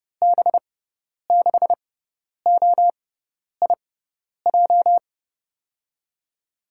Morse code, heard as a keyed tone, about 31 wpm, spaced as for 10 wpm, 700 Hz. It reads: B6OIJ
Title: USA Call Signs - 6 Characters - 6x Character Spacing - 30wpm